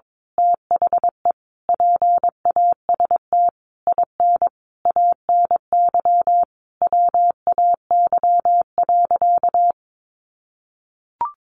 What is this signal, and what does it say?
Morse code, heard as a keyed tone, about 22 wpm, 700 Hz. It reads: THE PAST IN ANY WAY.
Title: A Sound Of Thunder 22wpm